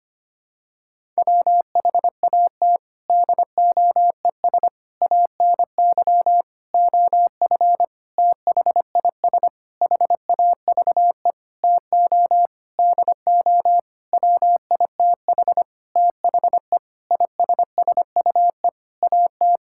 Morse code, heard as a keyed tone, about 25 wpm, 700 Hz. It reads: WHAT DOES ANY OF THIS HAVE TO DO WITH THE ISSUE AT
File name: practice / 25wpm / Sentences with 9-Word Phrases - 25wpm